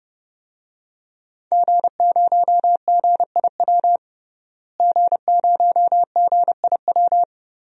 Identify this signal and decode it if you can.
Morse code, heard as a keyed tone, about 31 wpm, 700 Hz. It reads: G0GIW G0GIW
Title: DXCC-100 Call Signs - 5 Characters - Repeated 3x - 30wpm